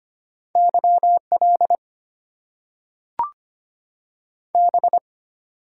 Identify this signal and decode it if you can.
Morse code, heard as a keyed tone, about 25 wpm, 700 Hz. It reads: YL B
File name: Top 90 QSO Elements - 8x Character Spacing - 25wpm